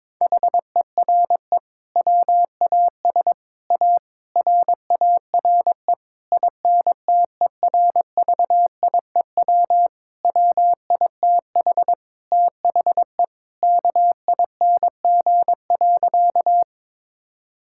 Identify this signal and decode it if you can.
Morse code, heard as a keyed tone, about 22 wpm, 700 Hz. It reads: HERE WAS A RARE INTERVIEW WITH THE KING.
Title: Sentences from Top 1600 Words 22wpm